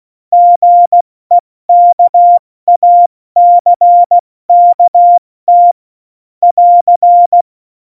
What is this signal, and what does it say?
Morse code, heard as a keyed tone, about 16 wpm, 700 Hz. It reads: GEKACKT <AR>
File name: Morsekurs / Witze / Witz 28. November 2017, 70 F80